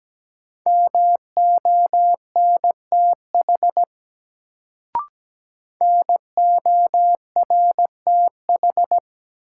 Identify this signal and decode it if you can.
Morse code, heard as a keyed tone, about 17 wpm, 700 Hz. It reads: MONTH NORTH
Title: Sets of 5 Words Encore - New Word Drill - 17wpm